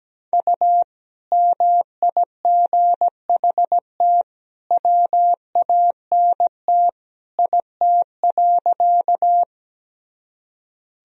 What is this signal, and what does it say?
Morse code, heard as a keyed tone, about 17 wpm, 700 Hz. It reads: U MIGHT WANT IT.
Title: Sentences from Top 300 Words and Abbreviations 17wpm